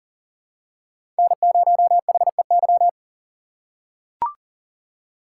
Morse code, heard as a keyed tone, about 37 wpm, 700 Hz. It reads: N0HEY
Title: USA Call Signs - 5 Characters - 2x Character Spacing - 40wpm